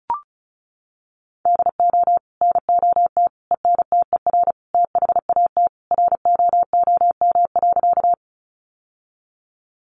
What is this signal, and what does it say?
Morse code, heard as a keyed tone, about 34 wpm, 700 Hz. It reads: DO NOT ENTER THAT ROOM.
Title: Sentences from Top 1300 Words 35wpm